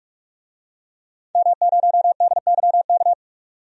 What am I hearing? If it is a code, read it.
Morse code, heard as a keyed tone, about 46 wpm, 700 Hz. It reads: M0DYK
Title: DXCC-100 Call Signs - 5 Characters - 45wpm